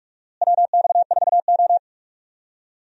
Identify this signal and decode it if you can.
Morse code, heard as a keyed tone, about 43 wpm, 700 Hz. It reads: WX4O
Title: USA Call Signs - 4 Characters - Repeated 2x - 45wpm